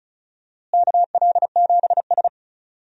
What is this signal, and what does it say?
Morse code, heard as a keyed tone, about 36 wpm, 700 Hz. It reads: KL7S
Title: DXCC-100 Call Signs - 4 Characters - Repeated 2x - 35wpm